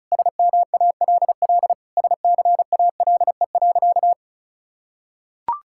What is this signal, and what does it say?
Morse code, heard as a keyed tone, about 36 wpm, 700 Hz. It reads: SMALL SCALE.
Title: Sentences from Top 1000 Words - Review Part 5 - 35wpm